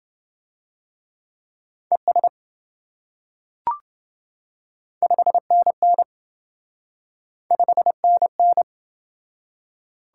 Morse code, heard as a keyed tone, about 27 wpm, 700 Hz. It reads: ES 5NN 5NN